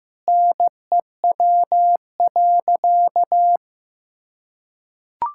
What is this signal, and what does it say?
Morse code, heard as a keyed tone, about 15 wpm, 700 Hz. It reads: NEW.
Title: Sentences from Top 1600 Words 15wpm